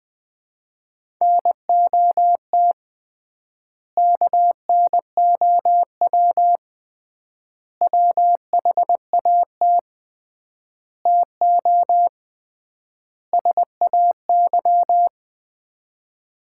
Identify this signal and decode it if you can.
Morse code, heard as a keyed tone, about 20 wpm, 700 Hz. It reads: NOT KNOW WHAT TO SAY
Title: 3x Word-Spacing - Sets of 5 Words - 20wpm